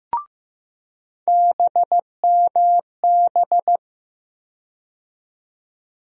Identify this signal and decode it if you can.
Morse code, heard as a keyed tone, about 15 wpm, 700 Hz. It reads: BMB
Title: USA Call Sign Postfixes - 3 Letters 15wpm